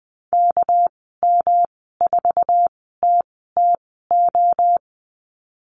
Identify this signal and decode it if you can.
Morse code, heard as a keyed tone, about 20 wpm, spaced as for 15 wpm, 700 Hz. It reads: KM4TTO
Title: 207 Lesson 33g - Letter j - Call Signs 20wpm